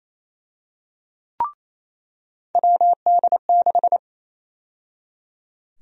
Morse code, heard as a keyed tone, about 28 wpm, 700 Hz. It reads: WD6
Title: USA Call Sign Prefixes - 28wpm